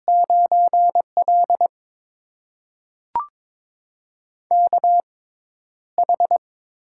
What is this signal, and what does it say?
Morse code, heard as a keyed tone, about 22 wpm, spaced as for 11 wpm, 700 Hz. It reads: 9L KH
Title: USA Call Signs - 6x Character Spacing - 22wpm